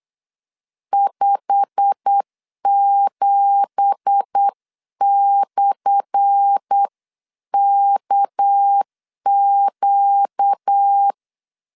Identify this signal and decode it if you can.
Morse code, heard as a keyed tone, about 8 wpm, 800 Hz. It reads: 57/KQ
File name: LEZIONI / 10 SIGNS / 10A SLASH 8wpm 3452